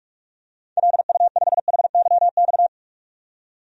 Morse code, heard as a keyed tone, about 45 wpm, 700 Hz. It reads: LU5HYX
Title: DXCC-100 Call Signs - 6 Characters - 45wpm